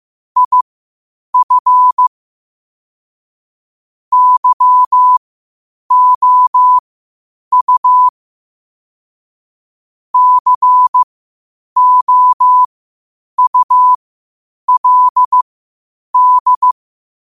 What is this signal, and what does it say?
Morse code, heard as a keyed tone, about 15 wpm, spaced as for 8 wpm, 1000 Hz. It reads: IF YOU COULD
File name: qotdfiles / 8wpm